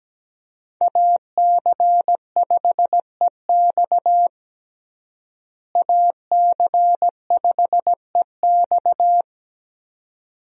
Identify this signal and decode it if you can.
Morse code, heard as a keyed tone, about 17 wpm, 700 Hz. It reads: AC5EX AC5EX
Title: USA Call Signs - 5 Characters - Repeated 2x - 17wpm